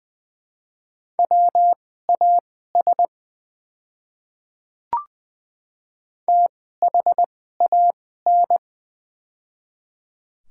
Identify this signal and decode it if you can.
Morse code, heard as a keyed tone, about 20 wpm, spaced as for 14 wpm, 700 Hz. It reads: WAS THAN
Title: 2x Character Spacing - Top 200 Words - 20wpm